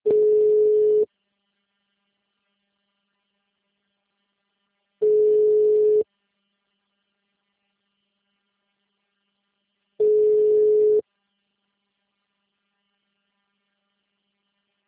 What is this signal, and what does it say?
Ringback tone (Europe, 425 Hz).
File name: out- -2019- - - .72729